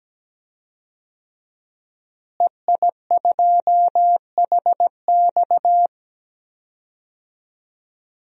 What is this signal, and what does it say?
Morse code, heard as a keyed tone, about 17 wpm, 700 Hz. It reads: EI2HX
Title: DXCC-100 Call Signs - 5 Characters - Repeated 2x - 17wpm